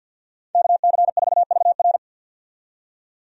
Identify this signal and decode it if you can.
Morse code, heard as a keyed tone, about 48 wpm, 700 Hz. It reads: KC4VR
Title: USA Call Signs - 5 Characters - 50wpm